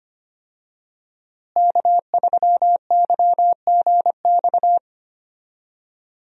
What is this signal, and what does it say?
Morse code, heard as a keyed tone, about 25 wpm, 700 Hz. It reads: K3YGX
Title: USA Call Signs - 5 Characters - Repeated 2x - 25wpm